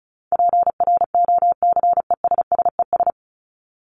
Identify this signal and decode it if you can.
Morse code, heard as a keyed tone, about 34 wpm, 700 Hz. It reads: PROCESSES
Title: ES:S Suffix - Top 100 Words with Base Words 35wpm